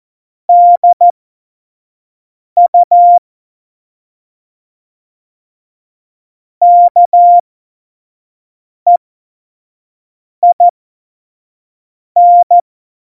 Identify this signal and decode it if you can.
Morse code, heard as a keyed tone, about 14 wpm, spaced as for 5 wpm, 700 Hz. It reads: DU KEIN
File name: Morsekurs / Witze / Witz 05. Juni 2018, 25 F70